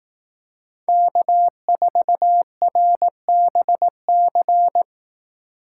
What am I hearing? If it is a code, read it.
Morse code, heard as a keyed tone, about 18 wpm, 700 Hz. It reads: K4RBC
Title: Speed Racing - USA Call Signs 15wpm